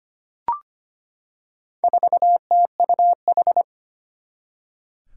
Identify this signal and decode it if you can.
Morse code, heard as a keyed tone, about 25 wpm, 700 Hz. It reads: 4TUH
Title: Four Challenging Letter-Number Combinations 25wpm